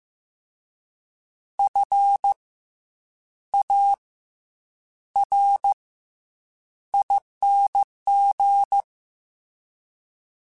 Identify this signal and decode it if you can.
Morse code, heard as a keyed tone, about 15 wpm, spaced as for 8 wpm, 800 Hz. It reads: FARING
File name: Special INGs-mp3s / Special INGs-mp3s / Fast ing @13